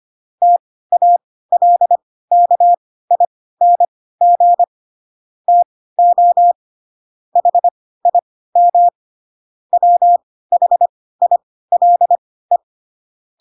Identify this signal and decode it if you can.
Morse code, heard as a keyed tone, about 25 wpm, spaced as for 16 wpm, 700 Hz. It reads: TALKING TO HIM WHILE
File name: Pinocchio / Pinocchio ch7 10 25 ik1hge